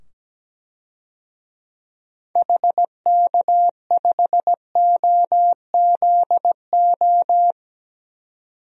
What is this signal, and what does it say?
Morse code, heard as a keyed tone, about 17 wpm, 700 Hz. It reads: HK5OZO
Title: DXCC-100 Call Signs - 6 Characters - 17wpm